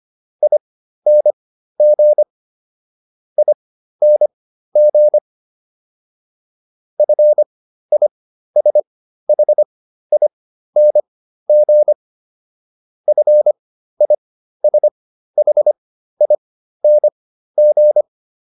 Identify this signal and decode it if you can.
Morse code, heard as a keyed tone, about 25 wpm, spaced as for 13 wpm, 600 Hz. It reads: ING ING FISHING FISHING